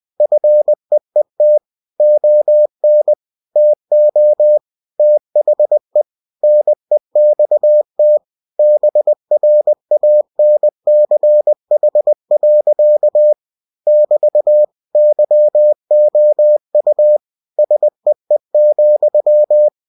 Morse code, heard as a keyed tone, about 20 wpm, 600 Hz. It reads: FEET ON TO THE NEXT BRANCH. <BT> YOU SEE,
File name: cw-books / winnie-the-pooh / winnie-20wpm-page-0003